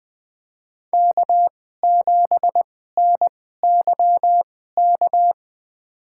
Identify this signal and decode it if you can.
Morse code, heard as a keyed tone, about 20 wpm, spaced as for 15 wpm, 700 Hz. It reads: K7NYK
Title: USA Call Signs - 2x Character Spacing - 20wpm